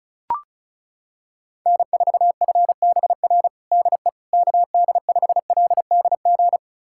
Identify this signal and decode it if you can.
Morse code, heard as a keyed tone, about 34 wpm, 700 Hz. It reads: N4FBR DE KD5LDG
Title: USA Callsign DE Callsign 35wpm